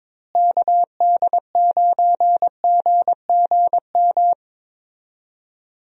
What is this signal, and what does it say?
Morse code, heard as a keyed tone, about 22 wpm, 700 Hz. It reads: KD9GGM
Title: USA Call Signs - 6 Characters - Repeated 3x - 22wpm